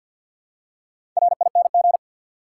Morse code, heard as a keyed tone, about 53 wpm, 700 Hz. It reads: RING
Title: Four Letter Words - 50wpm